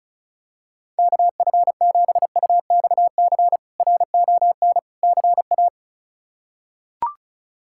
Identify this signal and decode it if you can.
Morse code, heard as a keyed tone, about 36 wpm, 700 Hz. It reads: KF7UXC RON CA
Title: K1USN SST QSO Practice - 35wpm